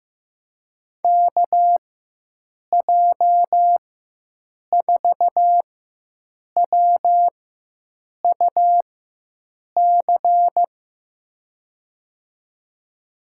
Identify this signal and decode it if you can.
Morse code, heard as a keyed tone, about 15 wpm, spaced as for 7 wpm, 700 Hz. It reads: KJ4WUC
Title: USA Call Signs - 6 Characters - 4x Character Spacing - 15wpm